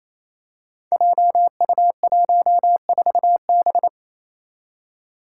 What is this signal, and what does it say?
Morse code, heard as a keyed tone, about 28 wpm, 700 Hz. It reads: JU14B
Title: Five Challenging Letter-Number Combinations 28wpm